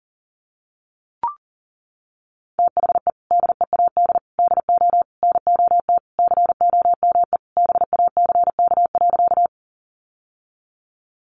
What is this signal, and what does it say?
Morse code, heard as a keyed tone, about 38 wpm, 700 Hz. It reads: THE DEAD DO NOT COME BACK.
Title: Sentences from Top 1500 Words 40wpm